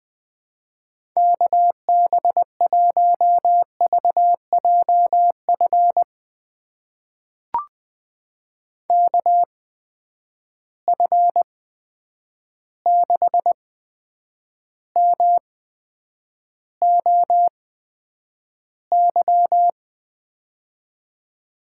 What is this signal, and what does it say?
Morse code, heard as a keyed tone, about 20 wpm, spaced as for 9 wpm, 700 Hz. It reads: KB1VJF KF6MOY
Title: USA Call Signs - 6 Characters - 8x Character Spacing - 20wpm